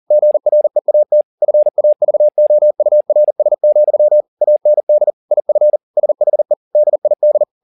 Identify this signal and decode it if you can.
Morse code, heard as a keyed tone, about 40 wpm, 600 Hz. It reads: GREAT FAVOURS, AND IF SHE DID